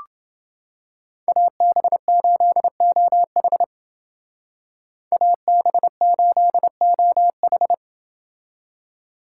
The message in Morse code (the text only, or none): AB8OH AB8OH